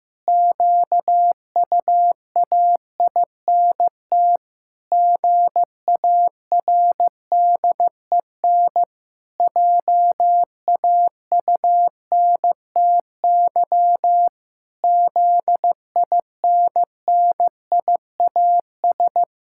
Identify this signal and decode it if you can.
Morse code, heard as a keyed tone, about 16 wpm, 700 Hz. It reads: QUAINT GARDEN JAUNTY ZINNIAS
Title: Pangram Challenge v2 - 15wpm